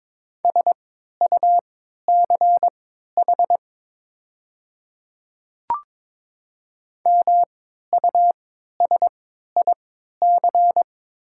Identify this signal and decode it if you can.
Morse code, heard as a keyed tone, about 22 wpm, spaced as for 12 wpm, 700 Hz. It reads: SUCH MUSIC
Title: 3x Character Spacing - Top 300 Words - 22wpm